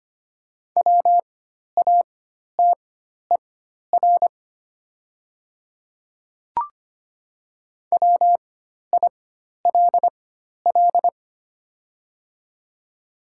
Morse code, heard as a keyed tone, about 25 wpm, spaced as for 12 wpm, 700 Hz. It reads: WATER WILL
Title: 4x Character Spacing - Top 200 Words - 25wpm